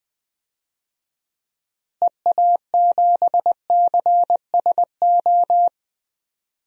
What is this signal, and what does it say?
Morse code, heard as a keyed tone, about 20 wpm, 700 Hz. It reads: EA7CSO
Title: DXCC-100 Call Signs - 6 Characters - Repeated 3x - 20wpm